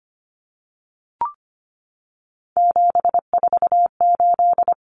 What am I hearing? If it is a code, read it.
Morse code, heard as a keyed tone, about 25 wpm, 700 Hz. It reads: 748
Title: Common Cut Numbers - 3-Digit 25wpm